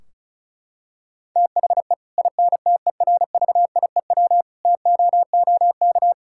Morse code, heard as a keyed tone, about 34 wpm, 700 Hz. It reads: THE INTERVIEW TOOK
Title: Sentences from Top 1600 Words 35wpm